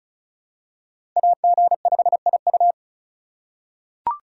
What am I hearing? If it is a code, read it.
Morse code, heard as a keyed tone, about 36 wpm, 700 Hz. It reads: AG5IU
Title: USA Call Signs - 5 Characters - 2x Character Spacing - 35wpm